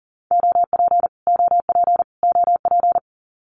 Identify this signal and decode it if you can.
Morse code, heard as a keyed tone, about 40 wpm, 700 Hz. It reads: OP OP OP
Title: Abbreviations Repeated 3x - 40wpm